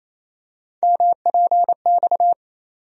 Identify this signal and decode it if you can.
Morse code, heard as a keyed tone, about 28 wpm, 700 Hz. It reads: MPX
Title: USA Call Sign Postfixes - 3 Letters 28wpm